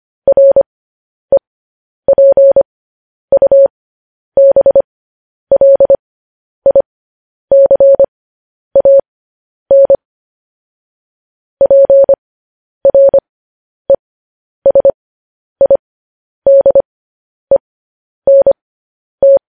Morse code, heard as a keyed tone, about 25 wpm, spaced as for 10 wpm, 550 Hz. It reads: REPUBLICAN PRESIDENT
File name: podcast / News headlines-1x-25x10